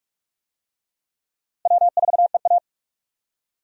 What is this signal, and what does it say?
Morse code, heard as a keyed tone, about 43 wpm, 700 Hz. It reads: W4EA
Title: USA Call Signs - 4 Characters - Repeated 3x - 45wpm